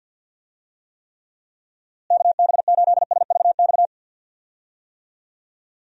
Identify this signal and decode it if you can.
Morse code, heard as a keyed tone, about 47 wpm, 700 Hz. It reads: KB8SVX
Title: USA Call Signs - 6 Characters - Repeated 2x - 50wpm